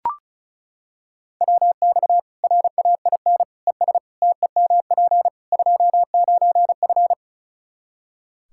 Morse code, heard as a keyed tone, about 34 wpm, 700 Hz. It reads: WX RAIN ES TEMP 29F
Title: Practice QSO - Exchange 2 - Info - 35wpm